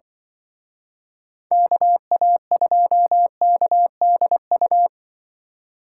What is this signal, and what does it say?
Morse code, heard as a keyed tone, about 25 wpm, 700 Hz. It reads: KA2KDU
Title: Speed Racing - USA 2x3 Call Signs 20wpm